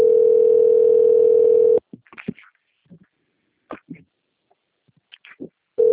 Ringback tone (North America), 440 + 480 Hz.